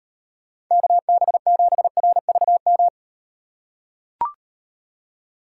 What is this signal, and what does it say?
Morse code, heard as a keyed tone, about 38 wpm, 700 Hz. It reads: KB7RVM